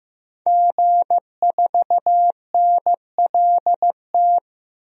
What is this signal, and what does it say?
Morse code, heard as a keyed tone, about 15 wpm, 700 Hz. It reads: G4NLT
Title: DXCC-100 Call Signs - 5 Characters - 15wpm